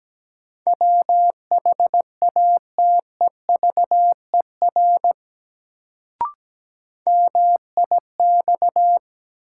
Morse code, heard as a keyed tone, about 17 wpm, 700 Hz. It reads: WHATEVER MIX
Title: Sentences from Top 1100 Words - New Word Drill Alt - 17wpm